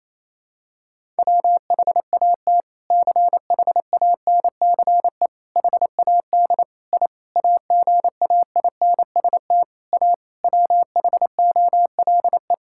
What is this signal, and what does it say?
Morse code, heard as a keyed tone, about 28 wpm, 700 Hz. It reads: WHAT CHANCE HAD I AGAINST A WHOLE
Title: Sentences from Top 1000 Words - Review Part 6 - 28wpm